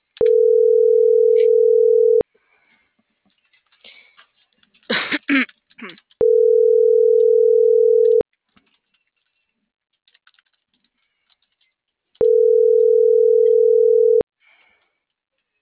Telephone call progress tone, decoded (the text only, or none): ringback tone